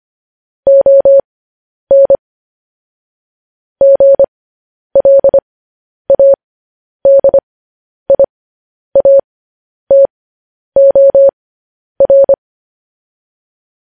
Morse code, heard as a keyed tone, about 25 wpm, spaced as for 10 wpm, 550 Hz. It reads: ON GLADIATOR
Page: news headlines in morse code at 10 wpm